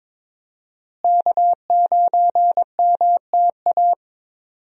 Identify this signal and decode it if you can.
Morse code, heard as a keyed tone, about 22 wpm, 700 Hz. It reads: K9MTA
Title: USA Call Signs - 5 Characters - Repeated 2x - 22wpm